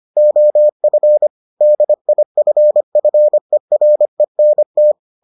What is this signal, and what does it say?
Morse code, heard as a keyed tone, about 25 wpm, 600 Hz. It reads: OF DIFFERENT